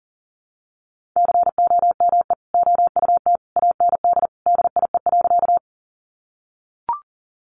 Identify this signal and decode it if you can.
Morse code, heard as a keyed tone, about 40 wpm, 700 Hz. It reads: COME OUT AND DIE.
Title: Sentences from Top 1300 Words 40wpm